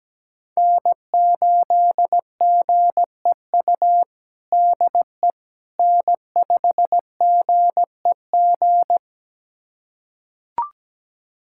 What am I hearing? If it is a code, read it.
Morse code, heard as a keyed tone, about 17 wpm, 700 Hz. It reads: N8GEU DE N5GEG
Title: USA Callsign DE Callsign - 5 Characters - 17wpm